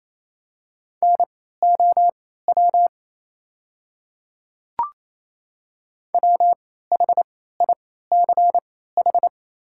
Morse code, heard as a keyed tone, about 28 wpm, spaced as for 16 wpm, 690 Hz. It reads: NOW WHICH
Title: 3x Character Spacing - Top 100 Words - 28wpm